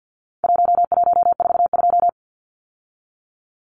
Morse code, heard as a keyed tone, about 50 wpm, 700 Hz. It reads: 1142
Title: Numbers - Contest Style - 4-Digit 50wpm